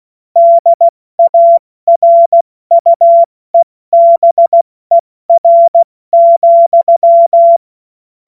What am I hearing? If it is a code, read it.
Morse code, heard as a keyed tone, about 16 wpm, 680 Hz. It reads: DARUEBER,
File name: Morsekurs / Witze / Witz 01. Februar 2022 70 F80